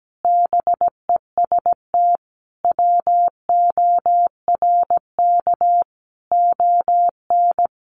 Morse code, heard as a keyed tone, about 17 wpm, 700 Hz. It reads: BEST WORK ON
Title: Sentences from Top 1200 Words 17wpm